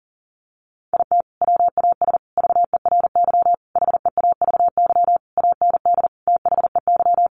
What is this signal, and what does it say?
Morse code, heard as a keyed tone, about 40 wpm, 700 Hz. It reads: IT WAS VERY HEAVY AND THEY
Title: Sentences from Top 1100 Words 40wpm